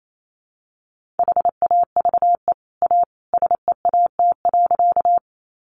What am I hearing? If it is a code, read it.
Morse code, heard as a keyed tone, about 28 wpm, 700 Hz. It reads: HAVE A SEAT.
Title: Sentences from Top 1500 Words 28wpm